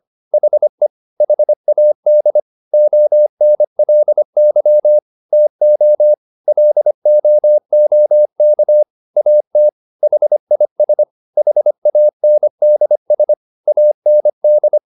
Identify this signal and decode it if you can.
Morse code, heard as a keyed tone, about 25 wpm, 600 Hz. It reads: HE HAD ONLY TO LOOK AT HIS HANDS AND